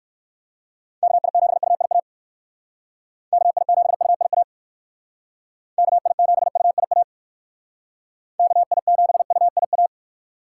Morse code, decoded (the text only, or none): KI7UIA KI7UIA KI7UIA KI7UIA